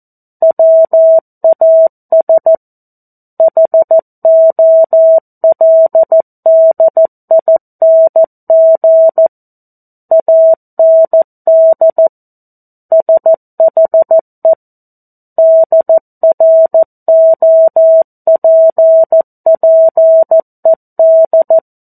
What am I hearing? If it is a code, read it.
Morse code, heard as a keyed tone, about 14 wpm, 650 Hz. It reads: WAS HOLDING AND SHE DROPPED
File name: 02-Chapter-02 14 WPM